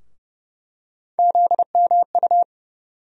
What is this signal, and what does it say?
Morse code, heard as a keyed tone, about 30 wpm, 700 Hz. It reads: ZMU
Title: USA Call Sign Postfixes - 3 Letters 30wpm